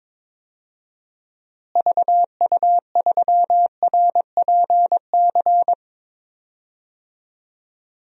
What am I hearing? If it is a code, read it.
Morse code, heard as a keyed tone, about 21 wpm, 700 Hz. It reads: VU3RPC
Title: DXCC-100 Call Signs - 6 Characters - Repeated 2x - 22wpm